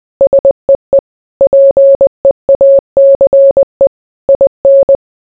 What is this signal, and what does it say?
Morse code, heard as a keyed tone, about 20 wpm, 550 Hz. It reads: SEE PEACE IN